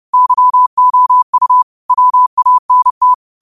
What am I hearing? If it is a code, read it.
Morse code, heard as a keyed tone, about 30 wpm, 1000 Hz. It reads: YOU WANT
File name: qotdfiles / 30wpm